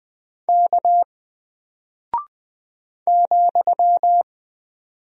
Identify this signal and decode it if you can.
Morse code, heard as a keyed tone, about 20 wpm, 700 Hz. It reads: K ,
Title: Single Character (Letter:Number:Punctuation:Prosign) Alt 20wpm